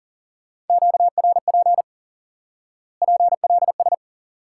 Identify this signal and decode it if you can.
Morse code, heard as a keyed tone, about 37 wpm, 700 Hz. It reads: QRP PLS
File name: IQE Course 3 - Lesson 89 - YAGI - Two Elements - 40wpm